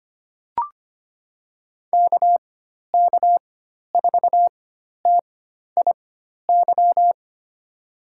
Morse code, heard as a keyed tone, about 25 wpm, spaced as for 12 wpm, 700 Hz. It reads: KK4TIY
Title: USA Call Signs - 6 Characters - 4x Character Spacing - 25wpm